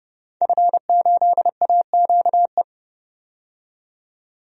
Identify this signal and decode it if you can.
Morse code, heard as a keyed tone, about 30 wpm, 700 Hz. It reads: F8AQE
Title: DXCC-100 Call Signs - 5 Characters - 30wpm